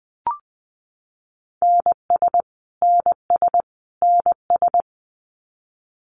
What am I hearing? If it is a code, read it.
Morse code, heard as a keyed tone, about 20 wpm, 700 Hz. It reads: NS NS NS
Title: Canadian Provinces and Territories - Repeated 3x - 20wpm